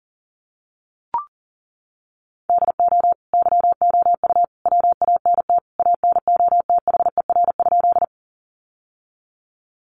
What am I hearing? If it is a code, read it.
Morse code, heard as a keyed tone, about 37 wpm, 700 Hz. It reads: DO YOU WANT ANOTHER?
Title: Sentences from Top 600 Words 40wpm